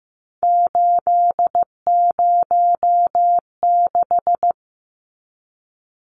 Morse code, heard as a keyed tone, about 15 wpm, 700 Hz. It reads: 806
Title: Numbers - 3-Digit 15wpm